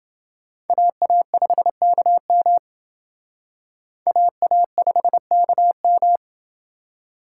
Morse code, heard as a keyed tone, about 28 wpm, 700 Hz. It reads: AA5KM AA5KM